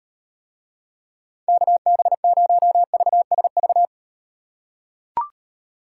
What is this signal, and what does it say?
Morse code, heard as a keyed tone, about 38 wpm, 700 Hz. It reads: KB0VSV